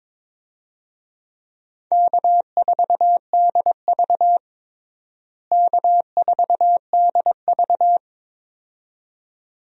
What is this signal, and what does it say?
Morse code, heard as a keyed tone, about 22 wpm, 700 Hz. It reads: K4DV K4DV
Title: USA Call Signs - 4 Characters - Repeated 2x - 22wpm